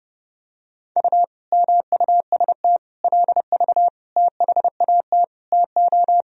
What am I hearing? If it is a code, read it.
Morse code, heard as a keyed tone, about 30 wpm, 700 Hz. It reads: U MUST LV THAT TO
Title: Sentences from Top 400 Words and Abbreviations 30wpm